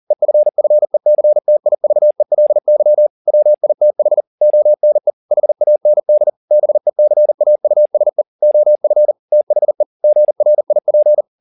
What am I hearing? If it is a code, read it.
Morse code, heard as a keyed tone, about 40 wpm, 600 Hz. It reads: EFFECTIVELY WITH ONE HAND BECAUSE OF THE GRIP
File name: the old man and the sea 2 40wpm